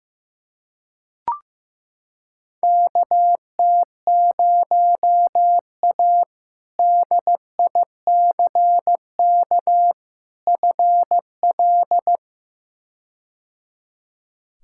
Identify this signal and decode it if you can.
Morse code, heard as a keyed tone, about 16 wpm, 700 Hz. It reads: KT0A DICK FL
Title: K1USN SST QSO Practice - 15wpm